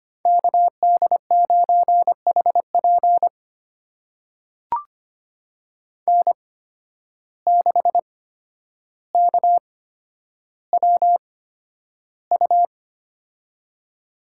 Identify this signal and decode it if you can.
Morse code, heard as a keyed tone, about 25 wpm, spaced as for 11 wpm, 700 Hz. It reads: KD9HP N6KWU
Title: USA Call Signs - 5 Characters - 8x Character Spacing - 25wpm